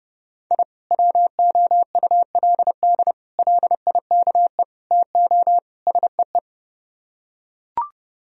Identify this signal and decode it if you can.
Morse code, heard as a keyed tone, about 31 wpm, 700 Hz. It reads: I WOULD LIKE TO SEE